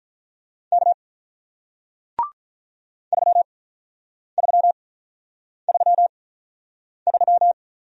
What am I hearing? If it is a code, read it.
Morse code, heard as a keyed tone, about 48 wpm, spaced as for 9 wpm, 700 Hz. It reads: K 3333